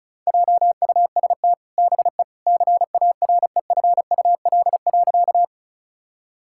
Morse code, heard as a keyed tone, about 36 wpm, 700 Hz. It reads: JUST BE CAREFUL.
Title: Sentences from Top 1600 Words 35wpm